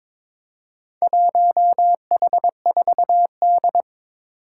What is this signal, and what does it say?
Morse code, heard as a keyed tone, about 23 wpm, 700 Hz. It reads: 1H4D
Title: Four Challenging Letter-Number Combinations 22wpm